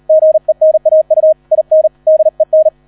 Morse code, heard as a keyed tone, about 38 wpm, 630 Hz. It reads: GENAU IN DEN